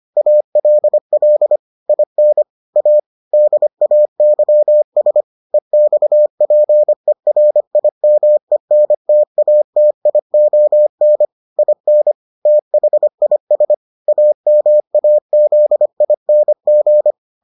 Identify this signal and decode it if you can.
Morse code, heard as a keyed tone, about 25 wpm, 600 Hz. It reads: ALL IN A DAYS EXPERIMENTATION IN THIS AMAZING